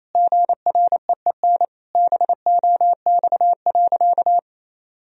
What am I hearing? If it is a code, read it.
Morse code, heard as a keyed tone, about 28 wpm, 700 Hz. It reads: GREEN BOX.
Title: Sentences from Top 1000 Words - Review Part 2 - 28wpm